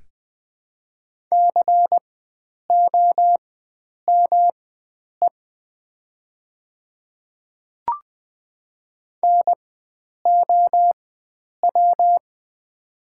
Morse code, heard as a keyed tone, about 20 wpm, spaced as for 9 wpm, 700 Hz. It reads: COME NOW
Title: 4x Character Spacing - Top 100 Words - 20wpm